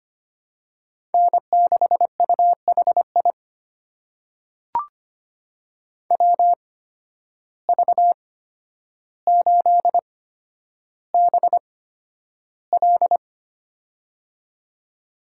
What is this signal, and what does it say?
Morse code, heard as a keyed tone, about 25 wpm, spaced as for 11 wpm, 700 Hz. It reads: N6UHI WV8BL
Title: USA Call Signs - 5 Characters - 8x Character Spacing - 25wpm